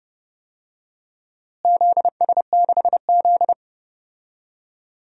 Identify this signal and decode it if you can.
Morse code, heard as a keyed tone, about 30 wpm, 700 Hz. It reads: ZS6Z
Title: DXCC-100 Call Signs - 4 Characters - Repeated 2x - 30wpm